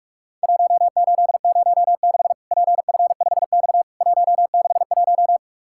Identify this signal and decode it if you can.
Morse code, heard as a keyed tone, about 44 wpm, 700 Hz. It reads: 1806 PF5X 161
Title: Worked All Europe DX Contest - QTC Exchange - 45wpm